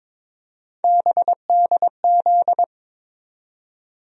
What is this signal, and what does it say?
Morse code, heard as a keyed tone, about 23 wpm, 700 Hz. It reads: BDZ
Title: USA Call Sign Postfixes - 3 Letters 22wpm